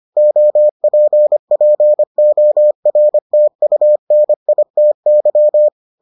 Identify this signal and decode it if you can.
Morse code, heard as a keyed tone, about 25 wpm, 600 Hz. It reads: OPPORTUNITY